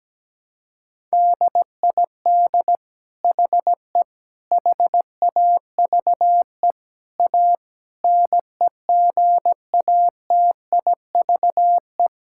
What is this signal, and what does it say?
Morse code, heard as a keyed tone, about 17 wpm, 700 Hz. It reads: DID HE HAVE A NEGATIVE
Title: Sentences from Top 1000 Words - Review Part 6 - 17wpm